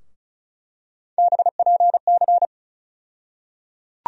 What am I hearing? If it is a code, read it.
Morse code, heard as a keyed tone, about 34 wpm, 700 Hz. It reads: BPC
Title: USA Call Sign Postfixes - 3 Letters 35wpm